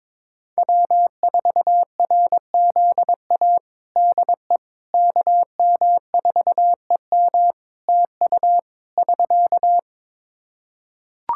Morse code, heard as a keyed tone, about 22 wpm, 700 Hz. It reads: W4RZA DE KM4EM TU <SK>
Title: Practice QSO - Exchange 4 - Ending - 22wpm